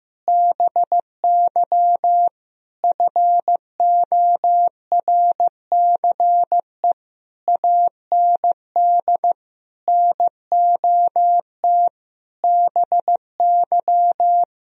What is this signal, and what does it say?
Morse code, heard as a keyed tone, about 15 wpm, 700 Hz. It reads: BY FORCE AND NOT BY
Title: Sentences from Top 1000 Words - Review Part 3 - 15wpm